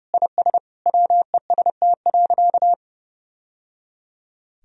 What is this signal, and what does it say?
Morse code, heard as a keyed tone, about 30 wpm, 700 Hz. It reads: IS WEST.
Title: Sentences from Top 1000 Words - Review Part 3 - 30wpm